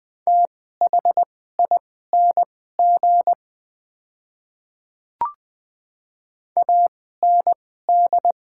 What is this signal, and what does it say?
Morse code, heard as a keyed tone, about 21 wpm, spaced as for 15 wpm, 700 Hz. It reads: THING AND
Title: 2x Character Spacing - Top 100 Words - 20wpm